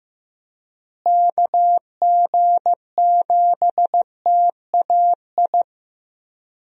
Morse code, heard as a keyed tone, about 15 wpm, 700 Hz. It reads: KG7TAI
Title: USA Call Signs - 6 Characters - Repeated 3x - 15wpm